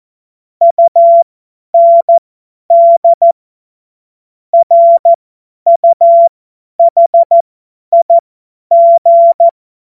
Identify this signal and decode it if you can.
Morse code, heard as a keyed tone, about 14 wpm, spaced as for 10 wpm, 680 Hz. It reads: UND RUHIG
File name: Morsekurs / Witze / Witz 08. Februar 2022 50 F70